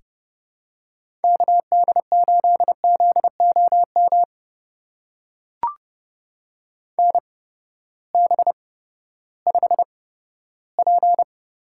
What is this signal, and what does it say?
Morse code, heard as a keyed tone, about 30 wpm, spaced as for 15 wpm, 700 Hz. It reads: KD8ZOM NB5P
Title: USA Call Signs - 8x Character Spacing - 30wpm